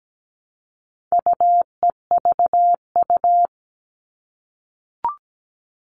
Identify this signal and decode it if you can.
Morse code, heard as a keyed tone, about 17 wpm, 700 Hz. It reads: UEVU
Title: All Cut Numbers - 4-Digit 17wpm